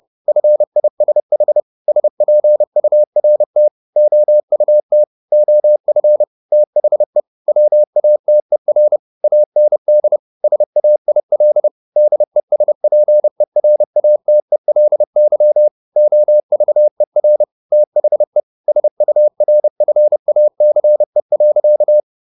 Morse code, heard as a keyed tone, about 30 wpm, 600 Hz. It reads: FISH SPURT OUT OF THE WATER AND SAIL DESPERATELY OVER THE SURFACE.
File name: the old man and the sea / the old man and the sea 1 30wpm